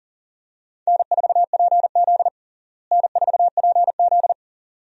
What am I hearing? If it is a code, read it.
Morse code, heard as a keyed tone, about 40 wpm, 700 Hz. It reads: N4PZ N4PZ
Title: USA Call Signs - 4 Characters - Repeated 2x - 40wpm